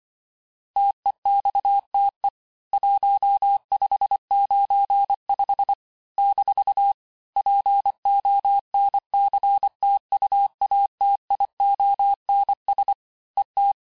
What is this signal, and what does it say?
Morse code, heard as a keyed tone, about 24 wpm, 800 Hz. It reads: TEXTE 1595 - PONCTUATIONS ET
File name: cours-du-lundi files / cours-1595-teste-PONCTUATIONS-vitesse18-MPM-puis-23-MPM